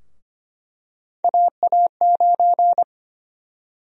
Morse code, heard as a keyed tone, about 25 wpm, 700 Hz. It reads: AA9
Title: USA Call Sign Prefixes - Repeated 2x - 25wpm